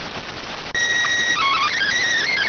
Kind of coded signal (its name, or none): SSTV (slow-scan television)